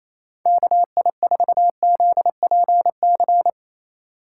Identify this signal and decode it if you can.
Morse code, heard as a keyed tone, about 28 wpm, 700 Hz. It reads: KI4ZPC
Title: USA Call Signs - 6 Characters - 4x Character Spacing - 28wpm